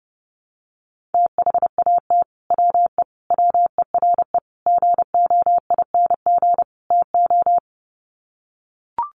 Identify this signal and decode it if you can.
Morse code, heard as a keyed tone, about 30 wpm, 700 Hz. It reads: THAT WE WERE GOING TO
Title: Sets of 5 Words Encore2 30wpm